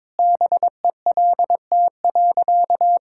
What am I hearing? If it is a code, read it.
Morse code, heard as a keyed tone, about 23 wpm, 700 Hz. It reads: BELT.
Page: morse-code-ninja-practice-22wpm